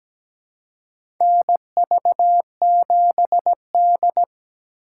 Morse code, heard as a keyed tone, about 17 wpm, 700 Hz. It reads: NV7D
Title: USA Call Signs - 8x Character Spacing - 17wpm